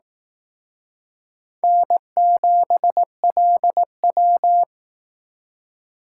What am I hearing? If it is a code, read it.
Morse code, heard as a keyed tone, about 18 wpm, 700 Hz. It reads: N7LW
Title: Speed Racing - USA 1x2 Call Signs 15wpm